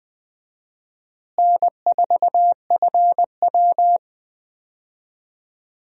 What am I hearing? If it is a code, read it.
Morse code, heard as a keyed tone, about 20 wpm, 700 Hz. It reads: N4FW
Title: USA Call Signs - 4 Characters - Repeated 3x - 20wpm